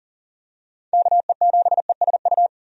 Morse code, heard as a keyed tone, about 40 wpm, 690 Hz. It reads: KE7ESU
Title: USA Call Signs - 6 Characters - 40wpm